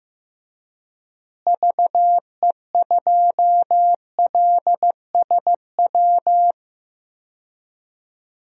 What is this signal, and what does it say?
Morse code, heard as a keyed tone, about 15 wpm, 700 Hz. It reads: VE2LSW
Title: DXCC-100 Call Signs - 6 Characters - Repeated 2x - 15wpm